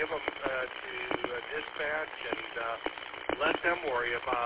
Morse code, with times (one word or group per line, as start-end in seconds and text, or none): none